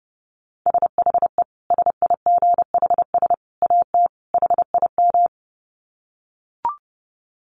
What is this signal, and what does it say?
Morse code, heard as a keyed tone, about 30 wpm, 700 Hz. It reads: SHE SIGHS AT HIM
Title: Dit Challenge 30wpm